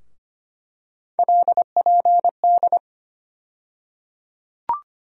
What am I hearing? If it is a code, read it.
Morse code, heard as a keyed tone, about 25 wpm, 700 Hz. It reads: LPD
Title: USA Call Sign Postfixes - 3 Letters 25wpm